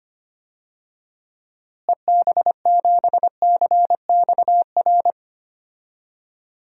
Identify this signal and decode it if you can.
Morse code, heard as a keyed tone, about 25 wpm, 700 Hz. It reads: EB7CXR
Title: DXCC-100 Call Signs - 6 Characters - Repeated 3x - 25wpm